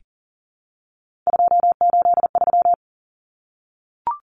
Morse code, heard as a keyed tone, about 40 wpm, 700 Hz. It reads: 283
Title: Numbers - 3-Digit 40wpm